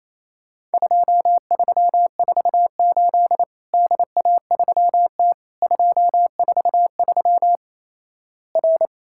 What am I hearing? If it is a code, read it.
Morse code, heard as a keyed tone, about 28 wpm, 700 Hz. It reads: 2348 DA3T 243 R
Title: Worked All Europe DX Contest - QTC Exchange - 28wpm